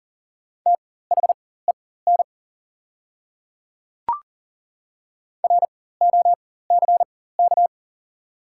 Morse code, heard as a keyed tone, about 40 wpm, spaced as for 19 wpm, 690 Hz. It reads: THEN ROCK
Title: 4x Character Spacing - Top 500 Words - 40wpm